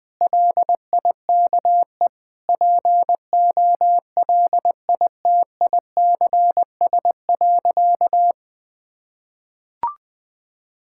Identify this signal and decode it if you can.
Morse code, heard as a keyed tone, about 21 wpm, 700 Hz. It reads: LIKE POLITICS.
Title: Sentences from Top 1000 Words - Review Part 6 - 20wpm